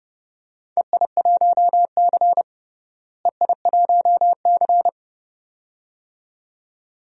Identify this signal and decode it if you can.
Morse code, heard as a keyed tone, about 30 wpm, 700 Hz. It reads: EI1C EI1C
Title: DXCC-100 Call Signs - 4 Characters - Repeated 2x - 30wpm